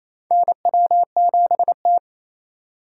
Morse code, heard as a keyed tone, about 28 wpm, 700 Hz. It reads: NW7T
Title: USA Call Signs - 2x Character Spacing - 28wpm